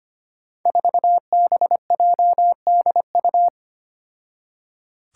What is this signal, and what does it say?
Morse code, heard as a keyed tone, about 25 wpm, 700 Hz. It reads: 4BJDU
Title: Five Challenging Letter-Number Combinations 25wpm